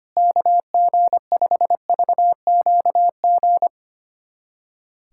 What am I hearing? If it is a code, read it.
Morse code, heard as a keyed tone, about 25 wpm, 700 Hz. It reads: KG5VQG
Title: USA Callsign DE Callsign 25wpm